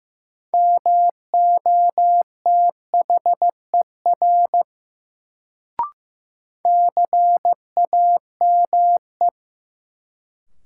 Morse code, heard as a keyed tone, about 16 wpm, 700 Hz. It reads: MOTHER CAME
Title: Sentences from Top 200 Words - New Word Drill - 15wpm